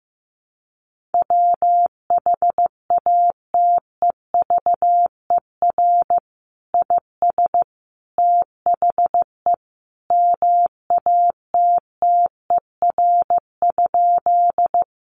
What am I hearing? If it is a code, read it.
Morse code, heard as a keyed tone, about 14 wpm, 700 Hz. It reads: WHATEVER IS THE MATTER?
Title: Sentences from Top 1100 Words 15wpm